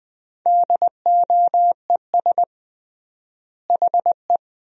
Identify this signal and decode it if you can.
Morse code, heard as a keyed tone, about 20 wpm, 700 Hz. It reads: DOES HE
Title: 3x Word-Spacing - Sentences from Top 400 Words - 20wpm